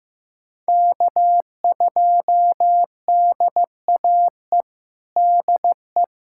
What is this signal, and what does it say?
Morse code, heard as a keyed tone, about 15 wpm, 700 Hz. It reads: K2DAE DE
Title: USA Callsign DE Callsign 15wpm